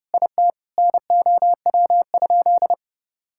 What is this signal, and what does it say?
Morse code, heard as a keyed tone, about 30 wpm, 700 Hz. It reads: IT NOW?
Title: Sentences from Top 1000 Words - Review Part 5 - 30wpm